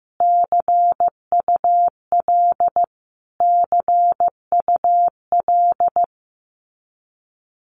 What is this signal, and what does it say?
Morse code, heard as a keyed tone, about 15 wpm, 700 Hz. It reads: CUL CUL
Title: Abbreviations Repeated 3x - 15wpm